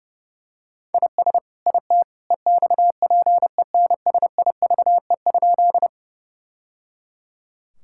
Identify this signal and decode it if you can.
Morse code, heard as a keyed tone, about 30 wpm, 700 Hz. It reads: IS IT EXPENSIVE?
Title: Sentences from Top 1000 Words - Review Part 6 - 30wpm